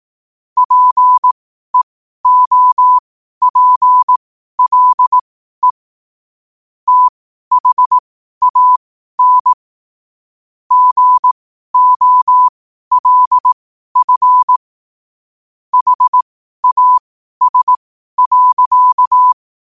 Morse code, heard as a keyed tone, about 18 wpm, spaced as for 12 wpm, 1000 Hz. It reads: PEOPLE THAN GOLF HAS.
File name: qotdfiles / 12wpm